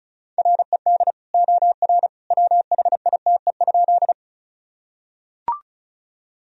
Morse code, heard as a keyed tone, about 36 wpm, 700 Hz. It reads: RED OR WHITE?
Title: Sentences from Top 1000 Words - Review Part 2 - 35wpm